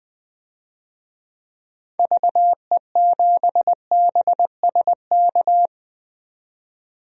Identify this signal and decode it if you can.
Morse code, heard as a keyed tone, about 20 wpm, 700 Hz. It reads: VE7BSK
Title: DXCC-100 Call Signs - 6 Characters - 20wpm